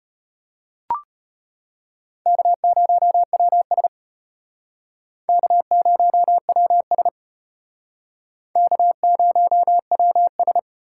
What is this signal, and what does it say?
Morse code, heard as a keyed tone, about 32 wpm, 700 Hz. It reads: K0WS K0WS K0WS